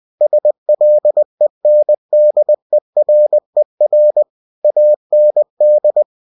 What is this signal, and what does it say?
Morse code, heard as a keyed tone, about 20 wpm, 600 Hz. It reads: SLENDERER AND
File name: cw-books / winnie-the-pooh / winnie-20wpm-page-0019